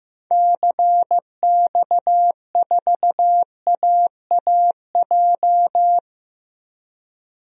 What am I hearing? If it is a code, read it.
Morse code, heard as a keyed tone, about 15 wpm, 700 Hz. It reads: CX4AAJ
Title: DXCC-100 Call Signs - 6 Characters - Repeated 3x - 15wpm